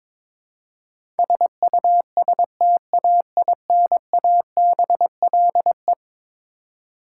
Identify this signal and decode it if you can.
Morse code, heard as a keyed tone, about 22 wpm, 700 Hz. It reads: SUSTAINABLE
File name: practice / 22wpm / Eleven Letter Words - 22wpm